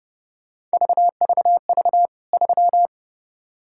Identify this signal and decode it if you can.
Morse code, heard as a keyed tone, about 29 wpm, 700 Hz. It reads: VVV 3
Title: K1USN SST QSO Practice 30wpm